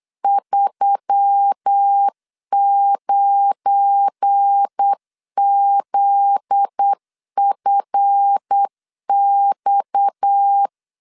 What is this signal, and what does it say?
Morse code, heard as a keyed tone, about 8 wpm, 800 Hz. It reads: 39ZFX
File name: LEZIONI / 8 YJ45 / 8 YJ45 8wpm 3214